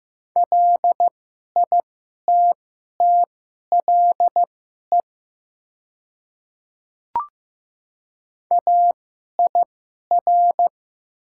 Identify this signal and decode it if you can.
Morse code, heard as a keyed tone, about 15 wpm, spaced as for 11 wpm, 700 Hz. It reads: LITTLE AIR
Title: 2x Character Spacing - Top 500 Words - 15wpm